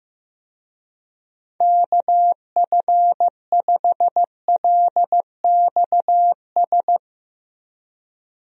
Morse code, heard as a keyed tone, about 15 wpm, 700 Hz. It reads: KF5LXS
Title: USA Call Signs - 6 Characters - Repeated 2x - 15wpm